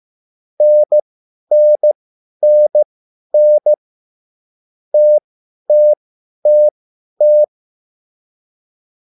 Morse code, heard as a keyed tone, about 15 wpm, spaced as for 10 wpm, 600 Hz. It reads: NNNN TTTT
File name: QSL / CW lesson